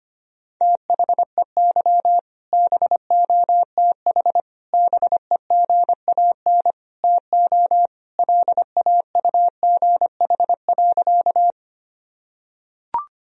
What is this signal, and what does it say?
Morse code, heard as a keyed tone, about 25 wpm, 700 Hz. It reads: THEY BOTH BEGAN TO LAUGH.
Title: Sentences from Top 1000 Words - Review Part 1 - 25wpm